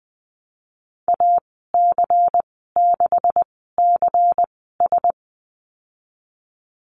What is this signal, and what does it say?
Morse code, heard as a keyed tone, about 20 wpm, spaced as for 14 wpm, 700 Hz. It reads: AC6CS
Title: 161 Lesson 26g - Letter g - Call Signs 20wpm